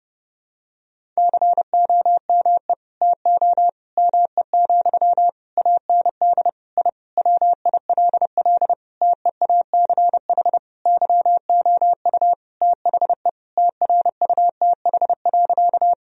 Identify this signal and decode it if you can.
Morse code, heard as a keyed tone, about 30 wpm, 700 Hz. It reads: COME TO ME, AND I WILL TEACH YOU THE TRUTH.
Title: Sentences from Top 1000 Words - Review Part 2 - 30wpm